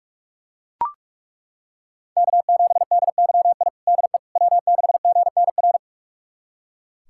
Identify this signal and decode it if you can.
Morse code, heard as a keyed tone, about 41 wpm, 700 Hz. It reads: K7DYI DE W6GNR
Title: USA Callsign DE Callsign - 5 Characters - 45wpm